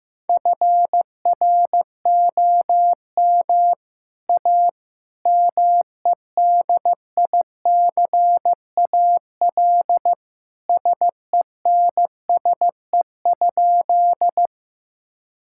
Morse code, heard as a keyed tone, about 15 wpm, 700 Hz. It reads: FROM A MEDICAL SENSE?
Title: Sentences from Top 1000 Words - Review Part 5 - 15wpm